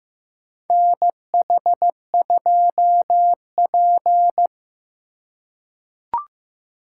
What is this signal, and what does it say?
Morse code, heard as a keyed tone, about 15 wpm, 700 Hz. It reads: NH2P
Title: USA Call Signs - 4 Characters - 4x Character Spacing - 15wpm